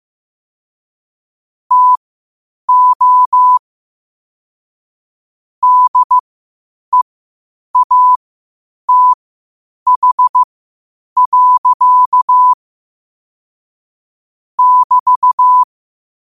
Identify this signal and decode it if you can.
Morse code, heard as a keyed tone, about 15 wpm, spaced as for 8 wpm, 1000 Hz. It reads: TO DEATH. <BT>
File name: qotdfiles / 8wpm